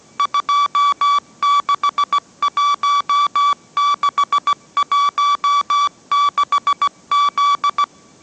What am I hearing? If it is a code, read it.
Morse code, heard as a keyed tone, about 17 wpm, 1150 Hz. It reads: 261616Z
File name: mrx / mrx080 18wpm